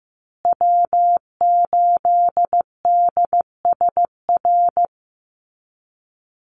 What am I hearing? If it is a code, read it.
Morse code, heard as a keyed tone, about 15 wpm, 700 Hz. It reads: W8DSR
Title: USA Call Signs Repeated 3x 15wpm